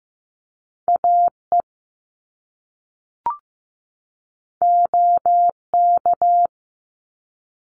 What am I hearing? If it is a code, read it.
Morse code, heard as a keyed tone, about 15 wpm, 700 Hz. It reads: AE OK
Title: Top Two Letter Combinations in Words 15wpm